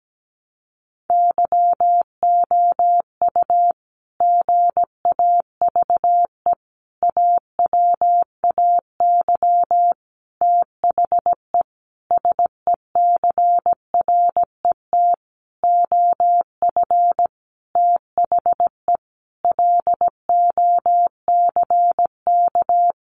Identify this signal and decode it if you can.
Morse code, heard as a keyed tone, about 17 wpm, 700 Hz. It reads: YOU GAVE AWAY THE SECRET OF THE LOCK
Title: Sentences from Top 1300 Words 17wpm